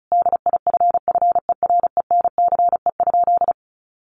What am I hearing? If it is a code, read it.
Morse code, heard as a keyed tone, about 34 wpm, 700 Hz. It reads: DIFFERENCE?
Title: Sentences from Top 800 Words 35wpm